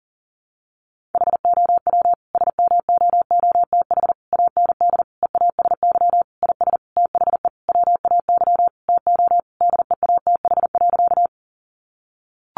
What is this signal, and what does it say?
Morse code, heard as a keyed tone, about 40 wpm, 700 Hz. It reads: HOW SMOOTH AND EASY IS THE WAY TO DEATH.
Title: Sentences from Top 1400 Words 40wpm